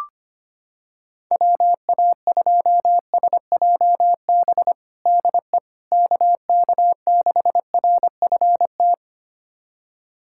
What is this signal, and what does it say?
Morse code, heard as a keyed tone, about 25 wpm, 700 Hz. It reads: WA2SJB DE KK6RFT
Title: USA Callsign DE Callsign - 6 Characters - 25wpm